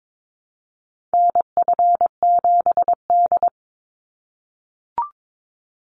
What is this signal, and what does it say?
Morse code, heard as a keyed tone, about 22 wpm, 700 Hz. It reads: NF7D
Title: Call Signs - 22wpm